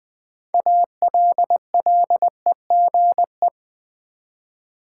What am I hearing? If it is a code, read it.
Morse code, heard as a keyed tone, about 20 wpm, 700 Hz. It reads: ALLEGE
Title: 6 Letter Words - First 3 Letters Given - 20wpm